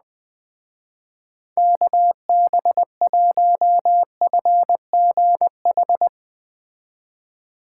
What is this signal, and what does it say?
Morse code, heard as a keyed tone, about 20 wpm, 700 Hz. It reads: KB1FGH
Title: Speed Racing - USA 2x3 Call Signs 17wpm